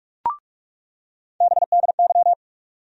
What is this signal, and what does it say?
Morse code, heard as a keyed tone, about 48 wpm, 700 Hz. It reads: BDY
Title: USA Call Sign Postfixes - 3 Letters 45wpm